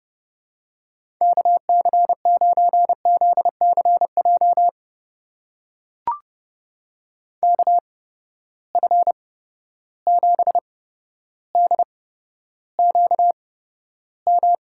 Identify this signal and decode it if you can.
Morse code, heard as a keyed tone, about 30 wpm, spaced as for 13 wpm, 700 Hz. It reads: KC9ZCJ KF7DQM
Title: USA Call Signs - 6 Characters - 8x Character Spacing - 30wpm